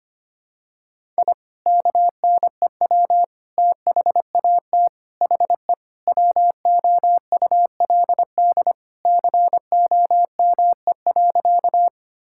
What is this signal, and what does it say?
Morse code, heard as a keyed tone, about 25 wpm, 700 Hz. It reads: I KNEW THAT HE WOULD COME.
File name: practice / 25wpm / Sentences with 2-Word Phrases - 25wpm